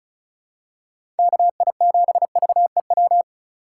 Morse code, heard as a keyed tone, about 36 wpm, 700 Hz. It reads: KI7VEW
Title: USA Call Signs - 3x Character Spacing - 35wpm